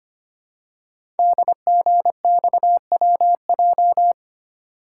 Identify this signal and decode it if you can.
Morse code, heard as a keyed tone, about 25 wpm, 700 Hz. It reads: DGXWJ
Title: Code Groups - 5 Letters - 25wpm